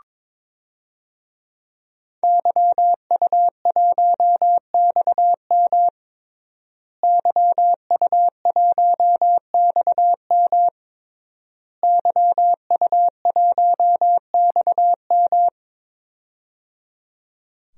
Morse code, heard as a keyed tone, about 22 wpm, 700 Hz. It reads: YU1XM YU1XM YU1XM
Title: DXCC-100 Call Signs - 5 Characters - Repeated 3x - 22wpm